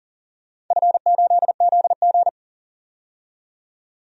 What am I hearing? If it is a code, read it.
Morse code, heard as a keyed tone, about 40 wpm, 700 Hz. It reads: F8ZG
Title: DXCC-100 Call Signs - 4 Characters - 40wpm